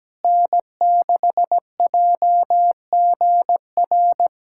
Morse code, heard as a keyed tone, about 17 wpm, 700 Hz. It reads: N6JGR
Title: USA Callsign DE Callsign - 5 Characters - 17wpm